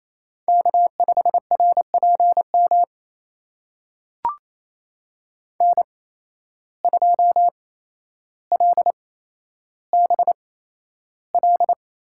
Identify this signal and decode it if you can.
Morse code, heard as a keyed tone, about 28 wpm, spaced as for 12 wpm, 700 Hz. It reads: K5RPM N2LBL
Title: USA Call Signs - 5 Characters - 8x Character Spacing - 28wpm